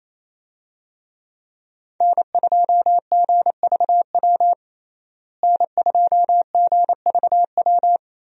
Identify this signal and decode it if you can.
Morse code, heard as a keyed tone, about 28 wpm, 700 Hz. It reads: N2GVW N2GVW
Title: USA Call Signs - 5 Characters - Repeated 3x - 28wpm